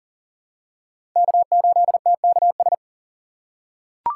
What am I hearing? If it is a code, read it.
Morse code, heard as a keyed tone, about 40 wpm, 700 Hz. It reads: K8TKS
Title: USA Call Signs - 5 Characters - 6x Character Spacing - 40wpm